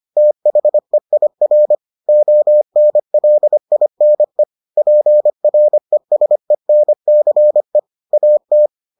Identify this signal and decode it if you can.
Morse code, heard as a keyed tone, about 25 wpm, 600 Hz. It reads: THEIR ONLINE PRESENCE AT